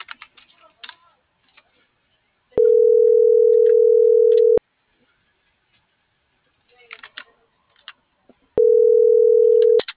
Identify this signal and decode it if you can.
Ringback tone (North America), 440 + 480 Hz.